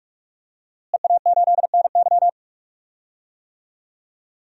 Morse code, heard as a keyed tone, about 44 wpm, 700 Hz. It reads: EA8NY
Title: DXCC-100 Call Signs - 5 Characters - Repeated 2x - 45wpm